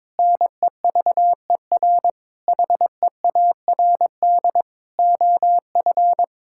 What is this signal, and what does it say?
Morse code, heard as a keyed tone, about 21 wpm, 700 Hz. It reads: NEVER HEARD OF
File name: Sentences with 9-Word Phrases - 22wpm